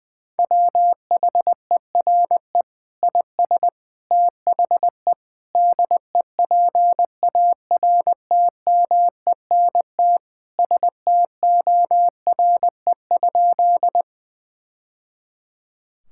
Morse code, heard as a keyed tone, about 20 wpm, 700 Hz. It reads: WHERE IS THE DEPARTMENT STORE?
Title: Sentences from Top 1000 Words - Review Part 5 - 20wpm